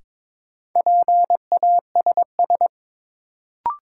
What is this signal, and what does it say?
Morse code, heard as a keyed tone, about 22 wpm, 700 Hz. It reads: PASS
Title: Sentences from Top 400 Words - New Word Drill - 22wpm